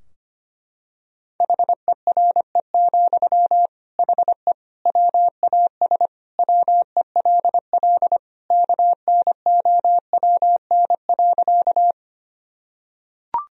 Morse code, heard as a keyed tone, about 25 wpm, 700 Hz. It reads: HERE, HE WAS WELL KNOWN.
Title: Sentences from Top 1000 Words - Review Part 3 - 25wpm